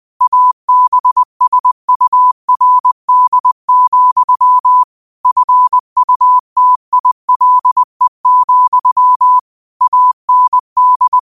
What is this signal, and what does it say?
Morse code, heard as a keyed tone, about 20 wpm, 1000 Hz. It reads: ABSURD, FUTILE, AND
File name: 20wpm